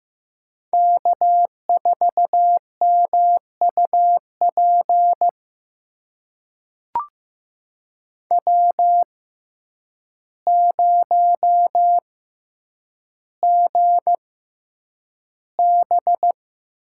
Morse code, heard as a keyed tone, about 15 wpm, spaced as for 8 wpm, 700 Hz. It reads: K4MUP W0GB
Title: USA Call Signs - 6x Character Spacing - 15wpm